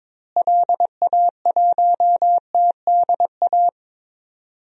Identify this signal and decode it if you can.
Morse code, heard as a keyed tone, about 22 wpm, 700 Hz. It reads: LA1TDA
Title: DXCC-100 Call Signs - 6 Characters - Repeated 2x - 22wpm